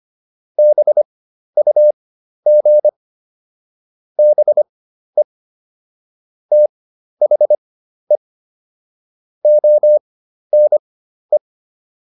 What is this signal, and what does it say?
Morse code, heard as a keyed tone, about 25 wpm, spaced as for 12 wpm, 600 Hz. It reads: BUG BE THE ONE